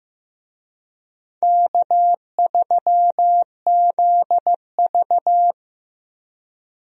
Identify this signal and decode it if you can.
Morse code, heard as a keyed tone, about 15 wpm, 700 Hz. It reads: K3ZV
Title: USA Call Signs - 4 Characters - 15wpm